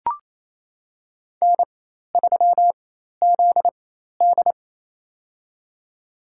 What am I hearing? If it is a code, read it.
Morse code, heard as a keyed tone, about 28 wpm, spaced as for 13 wpm, 700 Hz. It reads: N3ZD
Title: USA Call Signs - 4x Character Spacing - 28wpm